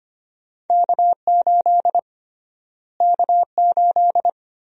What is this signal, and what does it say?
Morse code, heard as a keyed tone, about 25 wpm, 700 Hz. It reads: K8 K8
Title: USA Call Sign Prefixes - Repeated 2x - 25wpm